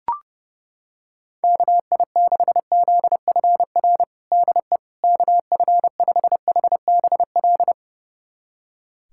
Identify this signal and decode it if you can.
Morse code, heard as a keyed tone, about 29 wpm, 700 Hz. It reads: KI6ZFR DE KF5HBL
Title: USA Callsign DE Callsign - 6 Characters - 30wpm